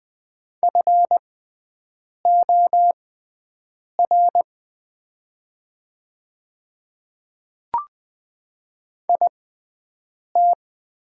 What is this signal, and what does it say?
Morse code, heard as a keyed tone, about 20 wpm, spaced as for 7 wpm, 700 Hz. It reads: FOR IT
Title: 6x Character Spacing - Top 100 Words - 20wpm